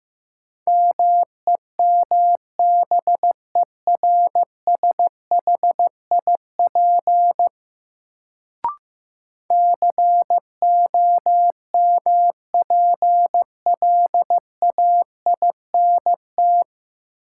Morse code, heard as a keyed tone, about 16 wpm, 700 Hz. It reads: MEMBERSHIP COMPLAINT
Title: Sentences from Top 1400 Words - New Word Drill Alt - 15wpm